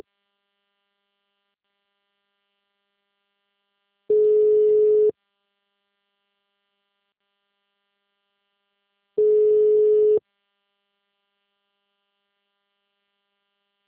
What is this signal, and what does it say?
Ringback tone (Europe, 425 Hz).
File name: out- -2042- - - .5127